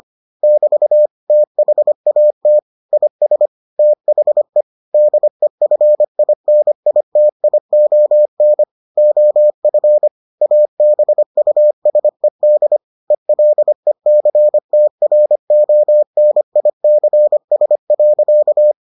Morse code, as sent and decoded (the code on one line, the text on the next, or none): -...- / - .... .- - / .. ... / - .... / -.. ..-. .. -. .. - .. --- -. / --- ..-. / .- -... ..- ... -.. / . .-.. -.-. - .-. --- -. .. -.-. ... .-.-.-
<BT> THAT IS THE DEFINITION OF ABUSED ELECTRONICS.